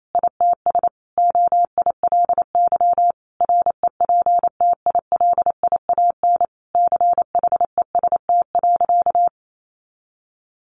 Morse code, heard as a keyed tone, about 28 wpm, 700 Hz. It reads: ITS OILY REPTILIAN CHEST.
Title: A Sound Of Thunder 28wpm